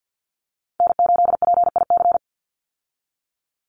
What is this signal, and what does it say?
Morse code, heard as a keyed tone, about 50 wpm, 700 Hz. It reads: N8PIC
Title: USA Call Signs 50wpm